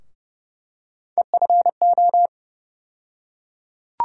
Morse code, heard as a keyed tone, about 30 wpm, 700 Hz. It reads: EFO
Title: USA Call Sign Postfixes - 3 Letters 30wpm